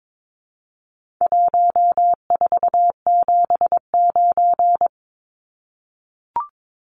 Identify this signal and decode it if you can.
Morse code, heard as a keyed tone, about 22 wpm, 700 Hz. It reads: 1479
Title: Numbers - Contest Style - 4-Digit 22wpm